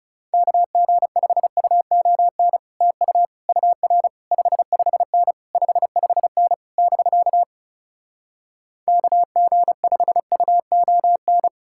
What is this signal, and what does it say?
Morse code, heard as a keyed tone, about 40 wpm, 700 Hz. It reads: KG5UON TU UR 55N 55N <BK> KG5UON